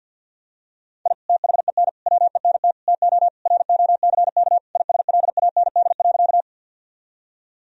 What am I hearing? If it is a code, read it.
Morse code, heard as a keyed tone, about 50 wpm, 700 Hz. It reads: I THEN WENT TO ROCK ISLAND.
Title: Sentences from Top 1000 Words - Review Part 3 - 50wpm